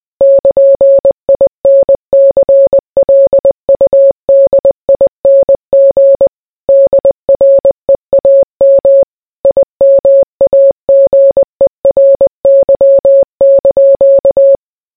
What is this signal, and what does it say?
Morse code, heard as a keyed tone, about 20 wpm, 550 Hz. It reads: <KN>INCLUDING DREAM IMAGERY)